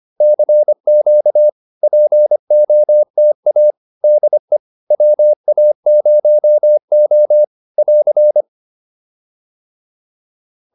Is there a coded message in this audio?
Yes: Morse code from 0.2 to 8.4 s, heard as a keyed tone, about 25 wpm, 600 Hz. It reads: CQ POTA DE WA0O <AR>